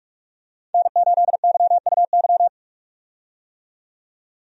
Morse code, heard as a keyed tone, about 47 wpm, 700 Hz. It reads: N8YUY
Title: USA Call Signs - 5 Characters - 45wpm